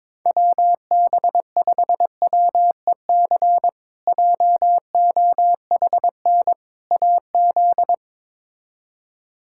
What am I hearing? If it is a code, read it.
Morse code, heard as a keyed tone, about 22 wpm, 700 Hz. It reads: WB5WEC JOHN AZ
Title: K1USN SST QSO Practice - 22wpm